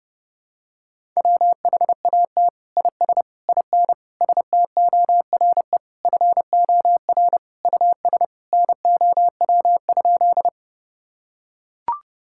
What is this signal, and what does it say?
Morse code, heard as a keyed tone, about 30 wpm, 700 Hz. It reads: WHAT IS IN STORE FOR US NOW?
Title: Sentences from Top 1000 Words - Review Part 5 - 30wpm